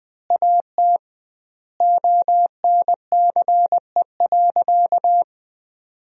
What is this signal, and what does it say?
Morse code, heard as a keyed tone, about 20 wpm, 700 Hz. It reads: AT ONCE.
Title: 2x Word-Spacing - Sentences from Top 300 Words - 20wpm